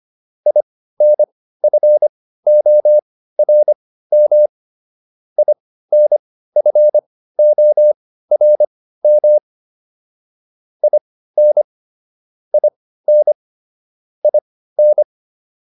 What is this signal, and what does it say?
Morse code, heard as a keyed tone, about 25 wpm, spaced as for 15 wpm, 600 Hz. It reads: INFORM INFORM IN IN IN